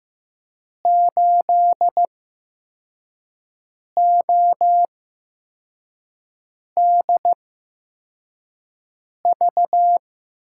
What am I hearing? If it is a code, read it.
Morse code, heard as a keyed tone, about 15 wpm, spaced as for 4 wpm, 700 Hz. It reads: 8ODV
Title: USA Call Signs - 8x Character Spacing - 15wpm